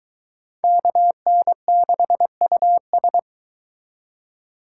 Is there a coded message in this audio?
Yes: Morse code from 0.6 to 3.2 s, heard as a keyed tone, about 23 wpm, 700 Hz. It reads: KN6US